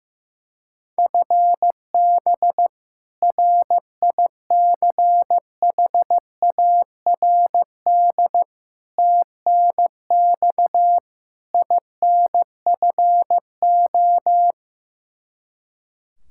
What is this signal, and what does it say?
Morse code, heard as a keyed tone, about 16 wpm, 700 Hz. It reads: FB RICHARD TNX INFO
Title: Practice QSO - Exchange 3 - More Info - 15wpm